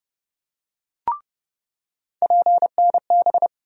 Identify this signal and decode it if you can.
Morse code, heard as a keyed tone, about 30 wpm, 700 Hz. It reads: PNB
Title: USA Call Sign Postfixes - 3 Letters 30wpm